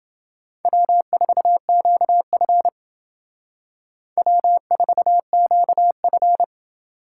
Morse code, heard as a keyed tone, about 28 wpm, 700 Hz. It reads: W4QF W4QF